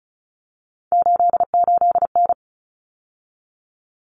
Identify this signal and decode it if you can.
Morse code, heard as a keyed tone, about 34 wpm, 700 Hz. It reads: 88N
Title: Common Cut Numbers - 3-Digit 35wpm